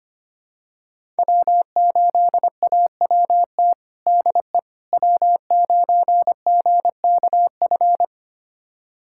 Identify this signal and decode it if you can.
Morse code, heard as a keyed tone, about 25 wpm, 700 Hz. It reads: W8AWT DE W9GKF
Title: USA Callsign DE Callsign - 5 Characters - 25wpm